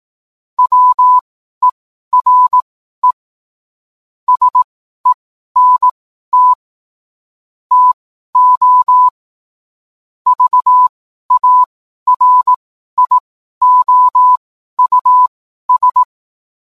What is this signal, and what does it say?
Morse code, heard as a keyed tone, about 18 wpm, spaced as for 12 wpm, 1000 Hz. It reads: WERE SENT TO VARIOUS